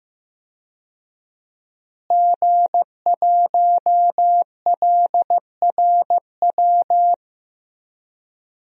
Morse code, heard as a keyed tone, about 15 wpm, 700 Hz. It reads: G1LRW
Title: DXCC-100 Call Signs - 5 Characters - Repeated 3x - 15wpm